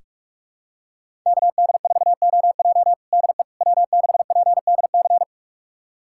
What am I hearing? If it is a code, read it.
Morse code, heard as a keyed tone, about 46 wpm, 700 Hz. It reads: KD4OJ DE W6PDC
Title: USA Callsign DE Callsign - 5 Characters - 45wpm